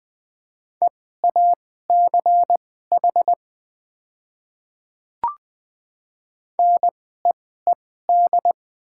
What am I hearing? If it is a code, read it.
Morse code, heard as a keyed tone, about 20 wpm, spaced as for 14 wpm, 700 Hz. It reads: EACH NEED
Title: 2x Character Spacing - Top 300 Words - 20wpm